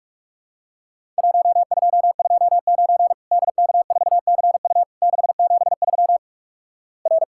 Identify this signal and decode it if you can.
Morse code, heard as a keyed tone, about 46 wpm, 700 Hz. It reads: 1229 DK4CU 673 R
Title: Worked All Europe DX Contest - QTC Exchange - 45wpm